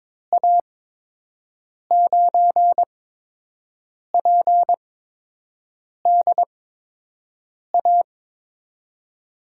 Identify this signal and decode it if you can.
Morse code, heard as a keyed tone, about 22 wpm, spaced as for 6 wpm, 700 Hz. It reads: A9PDA
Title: USA Call Signs - 6 Characters - 8x Character Spacing - 22wpm